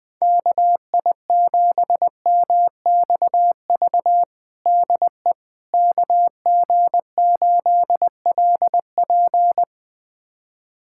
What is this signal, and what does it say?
Morse code, heard as a keyed tone, about 21 wpm, 700 Hz. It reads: KI7MXV DE KG8LP
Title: USA Callsign DE Callsign 20wpm